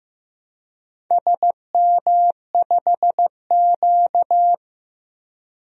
Morse code, heard as a keyed tone, about 15 wpm, 700 Hz. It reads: SM5Q
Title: DXCC-100 Call Signs - 4 Characters - 15wpm